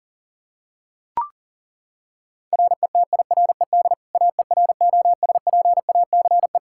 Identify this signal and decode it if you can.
Morse code, heard as a keyed tone, about 37 wpm, 700 Hz. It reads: RETIRED AEROSPACE
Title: K1USN SST QSO Practice 40wpm